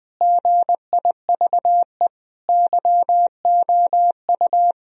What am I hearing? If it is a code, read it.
Morse code, heard as a keyed tone, about 21 wpm, 700 Hz. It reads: GIVE YOU
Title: Sentences from Top 1000 Words - Review Part 1 - 20wpm